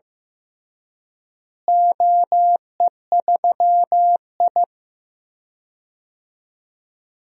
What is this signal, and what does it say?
Morse code, heard as a keyed tone, about 15 wpm, 700 Hz. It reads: OE3I
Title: DXCC-100 Call Signs - 4 Characters - Repeated 2x - 15wpm